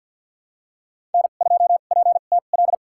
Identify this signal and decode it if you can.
Morse code, heard as a keyed tone, about 53 wpm, spaced as for 37 wpm, 700 Hz. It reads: N2PTL
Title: USA Call Signs - 5 Characters - 2x Character Spacing - 50wpm